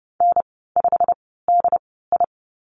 Morse code, heard as a keyed tone, about 30 wpm, spaced as for 17 wpm, 690 Hz. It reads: N5DI
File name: Lesson 15g - Number 5 - Call Signs 30wpm